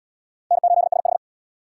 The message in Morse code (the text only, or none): N8SH